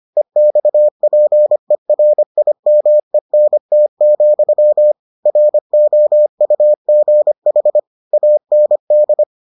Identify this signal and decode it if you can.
Morse code, heard as a keyed tone, about 25 wpm, 600 Hz. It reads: EXPERIMENT, ROUGH AND